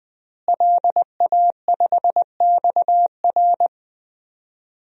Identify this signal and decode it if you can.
Morse code, heard as a keyed tone, about 20 wpm, 700 Hz. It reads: LA5XR
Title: DXCC-100 Call Signs - 5 Characters - Repeated 3x - 20wpm